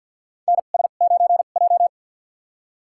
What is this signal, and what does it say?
Morse code, heard as a keyed tone, about 47 wpm, spaced as for 35 wpm, 700 Hz. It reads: NS9J
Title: USA Call Signs - 2x Character Spacing - 50wpm